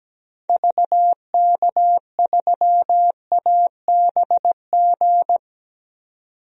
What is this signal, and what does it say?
Morse code, heard as a keyed tone, about 17 wpm, 700 Hz. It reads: VK3ABG
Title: DXCC-100 Call Signs - 6 Characters - 17wpm